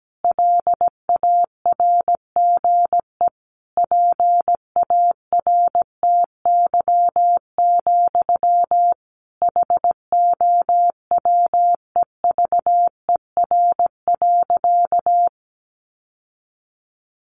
Morse code, heard as a keyed tone, about 17 wpm, 700 Hz. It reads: LARGE PARTY, HOWEVER.
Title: Sentences from Top 1400 Words 17wpm